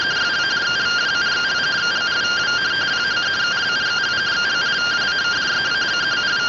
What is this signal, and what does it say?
Two-tone teleprinter signal, about 170 Hz shift.